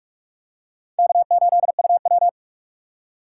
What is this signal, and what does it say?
Morse code, heard as a keyed tone, about 44 wpm, 700 Hz. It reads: K8UW
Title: USA Call Signs - 3x Character Spacing - 45wpm